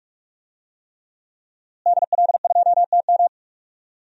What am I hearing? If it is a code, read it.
Morse code, heard as a keyed tone, about 46 wpm, 700 Hz. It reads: DL2TM
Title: DXCC-100 Call Signs - 5 Characters - Repeated 3x - 45wpm